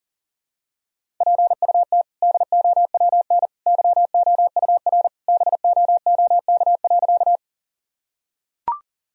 Morse code, heard as a keyed tone, about 40 wpm, 700 Hz. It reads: PUT DOWN YOUR BOOK.
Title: Sentences from Top 1000 Words - Review Part 1 - 40wpm